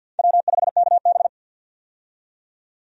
Morse code, heard as a keyed tone, about 53 wpm, 700 Hz. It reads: W5KB
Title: USA Call Signs - 4 Characters - 50wpm